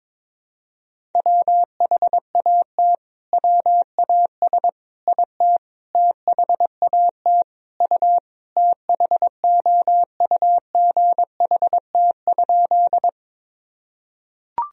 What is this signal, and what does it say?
Morse code, heard as a keyed tone, about 22 wpm, 700 Hz. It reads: WHAT WAS IT THAT U THOUGHT?
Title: Sentences from Top 300 Words and Abbreviations 22wpm